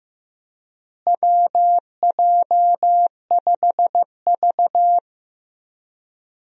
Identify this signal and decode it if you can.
Morse code, heard as a keyed tone, about 15 wpm, 700 Hz. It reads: WJ5V
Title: USA Call Signs - 4 Characters - Repeated 3x - 15wpm